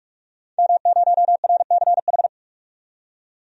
Morse code, heard as a keyed tone, about 43 wpm, 700 Hz. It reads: M0RCH
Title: DXCC-100 Call Signs - 5 Characters - 45wpm